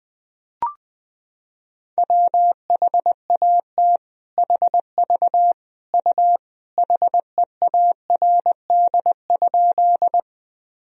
Morse code, heard as a keyed tone, about 20 wpm, 700 Hz. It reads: WHAT HV U HEARD?
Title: Sentences from Top 400 Words and Abbreviations 20wpm